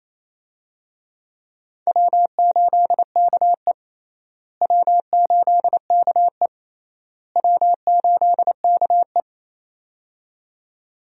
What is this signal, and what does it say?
Morse code, heard as a keyed tone, about 28 wpm, 700 Hz. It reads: W8KE W8KE W8KE
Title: USA Call Signs - 4 Characters - Repeated 3x - 28wpm